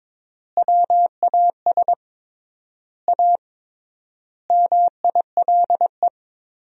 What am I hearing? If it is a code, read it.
Morse code, heard as a keyed tone, about 22 wpm, 700 Hz. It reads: WAS A MILE
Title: 3x Word-Spacing - Sentences from Top 500 Words - 22wpm